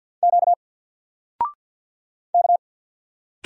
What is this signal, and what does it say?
Morse code, heard as a keyed tone, about 48 wpm, 700 Hz. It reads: Q K
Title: Single Character (Letter:Number:Punctuation:Prosign) 50wpm